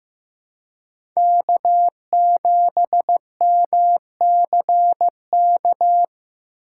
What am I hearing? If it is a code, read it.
Morse code, heard as a keyed tone, about 15 wpm, 700 Hz. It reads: K7MCK
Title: USA Call Signs - 5 Characters - 4x Character Spacing - 15wpm